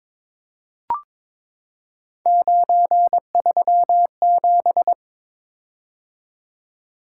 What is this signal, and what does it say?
Morse code, heard as a keyed tone, about 22 wpm, 700 Hz. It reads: 937
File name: practice / 22wpm / Three Letter-Number Combinations - 22wpm